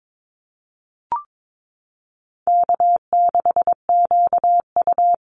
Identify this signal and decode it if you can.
Morse code, heard as a keyed tone, about 22 wpm, 700 Hz. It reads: K6QU
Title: Easy Call Signs Repeated 3x - 22wpm